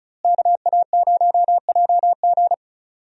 Morse code, heard as a keyed tone, about 35 wpm, 700 Hz. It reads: KA0JG
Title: USA Call Signs - 5 Characters - Repeated 3x - 35wpm